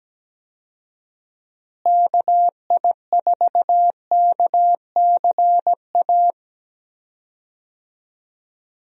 Morse code, heard as a keyed tone, about 17 wpm, 700 Hz. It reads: KI4KCA
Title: USA Call Signs - 6 Characters - Repeated 3x - 17wpm